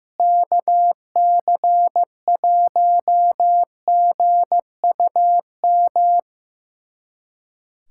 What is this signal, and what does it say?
Morse code, heard as a keyed tone, about 15 wpm, 700 Hz. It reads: KC1GUM
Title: USA Callsign DE Callsign - 6 Characters - 15wpm